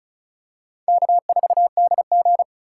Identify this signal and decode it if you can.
Morse code, heard as a keyed tone, about 34 wpm, 700 Hz. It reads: K4DG
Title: USA Call Signs - 4 Characters - 3x Character Spacing - 35wpm